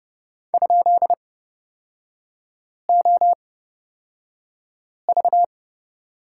Morse code, heard as a keyed tone, about 30 wpm, spaced as for 5 wpm, 700 Hz. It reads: ?OV
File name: Lesson 36e - BT - Three Characters 30wpm